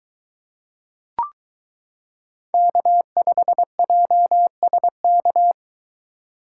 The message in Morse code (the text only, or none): K5JSK